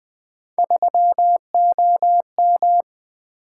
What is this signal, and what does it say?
Morse code, heard as a keyed tone, about 20 wpm, 700 Hz. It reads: 3OM
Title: Three Letter-Number Combinations - 20wpm